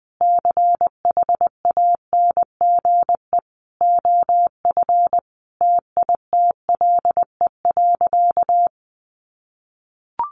Morse code, heard as a keyed tone, about 20 wpm, 700 Hz. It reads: CHANGE OF TITLE.
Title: Sentences from Top 1200 Words 20wpm